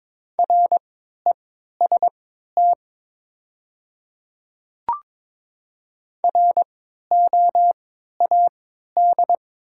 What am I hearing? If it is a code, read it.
Morse code, heard as a keyed tone, about 22 wpm, spaced as for 13 wpm, 700 Hz. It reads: REST ROAD
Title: 3x Character Spacing - Top 500 Words - 22wpm